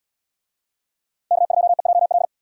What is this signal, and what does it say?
Morse code, heard as a keyed tone, about 73 wpm, 700 Hz. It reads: K2EJL